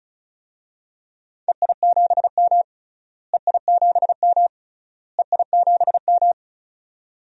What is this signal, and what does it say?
Morse code, heard as a keyed tone, about 34 wpm, 700 Hz. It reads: EI7M EI7M EI7M
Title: DXCC-100 Call Signs - 4 Characters - Repeated 3x - 35wpm